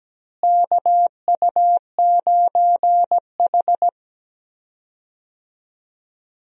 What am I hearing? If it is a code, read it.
Morse code, heard as a keyed tone, about 17 wpm, 700 Hz. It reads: KU9H
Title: USA Call Signs - 4 Characters - Repeated 2x - 17wpm